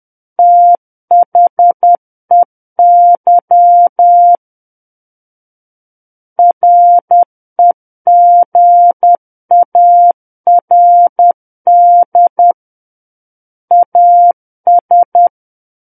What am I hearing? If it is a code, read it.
Morse code, heard as a keyed tone, about 10 wpm, 700 Hz. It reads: THEY REGARD AS
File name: Book One / 01-Chapter 1 - 10 WPM